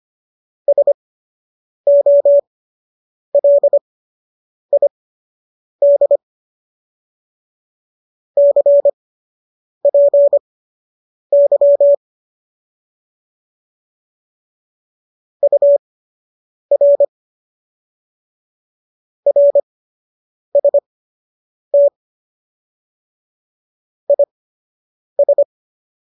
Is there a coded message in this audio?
Yes: Morse code from 0.7 to 25.4 s, heard as a keyed tone, about 25 wpm, spaced as for 8 wpm, 570 Hz. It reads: SOLID CPY UR RST IS